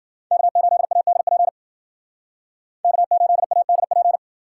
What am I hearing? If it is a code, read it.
Morse code, heard as a keyed tone, about 52 wpm, 700 Hz. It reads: K8ADP K8ADP